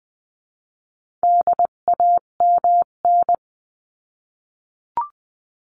Morse code, heard as a keyed tone, about 20 wpm, 700 Hz. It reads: DAMN
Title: 258 Lesson 42c - Words 16-20wpm